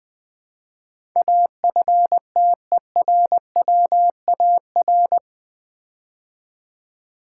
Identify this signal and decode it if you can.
Morse code, heard as a keyed tone, about 20 wpm, 700 Hz. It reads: AFTERWAR
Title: Build A Word - 4 Letters At A Time - 10-Letter Words - 20wpm